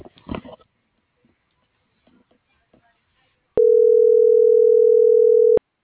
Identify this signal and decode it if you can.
Ringback tone (North America), 440 + 480 Hz.